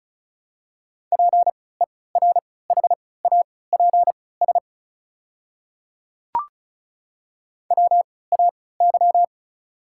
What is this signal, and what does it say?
Morse code, heard as a keyed tone, about 34 wpm, spaced as for 20 wpm, 690 Hz. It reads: PERHAPS WAY
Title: 3x Character Spacing - Top 500 Words - 35wpm